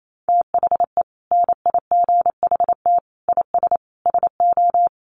Morse code, heard as a keyed tone, about 28 wpm, 700 Hz. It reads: THE NIGHT IS SO
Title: Sentences from Top 500 Words 28wpm